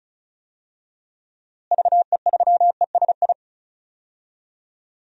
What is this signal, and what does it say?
Morse code, heard as a keyed tone, about 34 wpm, 700 Hz. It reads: VE3ESI
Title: DXCC-100 Call Signs - 6 Characters - Repeated 2x - 35wpm